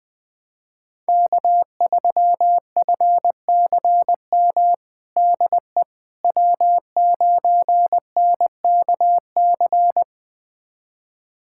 Morse code, heard as a keyed tone, about 21 wpm, 700 Hz. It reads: K3FCM DE W9NKC
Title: USA Callsign DE Callsign - 5 Characters - 20wpm